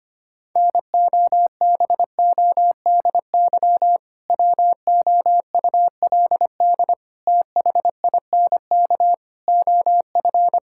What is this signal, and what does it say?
Morse code, heard as a keyed tone, about 25 wpm, 700 Hz. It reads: NOBODY WOULD THINK OF
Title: Sentences from Top 1600 Words 25wpm